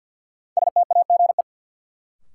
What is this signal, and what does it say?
Morse code, heard as a keyed tone, about 51 wpm, 700 Hz. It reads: STAGE
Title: Sentences from Top 1200 Words - New Word Drill Alt - 50wpm